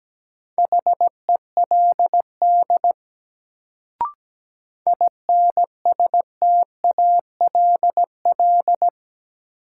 Morse code, heard as a keyed tone, about 17 wpm, 700 Hz. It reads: HELD INSTALL
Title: Sentences from Top 1600 Words - New Word Drill Alt - 17wpm